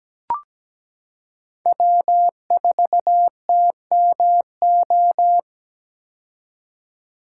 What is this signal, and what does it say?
Morse code, heard as a keyed tone, about 17 wpm, 700 Hz. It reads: W4TMO
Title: K1USN SST QSO Practice - 17wpm